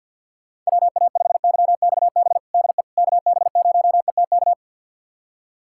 Morse code, heard as a keyed tone, about 50 wpm, 700 Hz. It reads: WA5YXB DE KB0ETK
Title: USA Callsign DE Callsign - 6 Characters - 50wpm